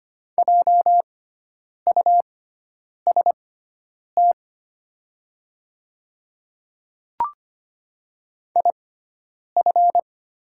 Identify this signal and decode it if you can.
Morse code, heard as a keyed tone, about 25 wpm, spaced as for 9 wpm, 700 Hz. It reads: JUST IF
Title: 6x Character Spacing - Top 200 Words - 25wpm